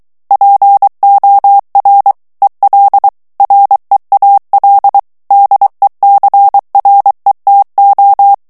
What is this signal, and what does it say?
Morse code, heard as a keyed tone, about 23 wpm, 800 Hz. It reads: POR EL REAL DECRETO
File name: morse mp3 / ESPAGNOL TEXTE EN CLAIR 2 VITESSE 22 MOTS MINUTES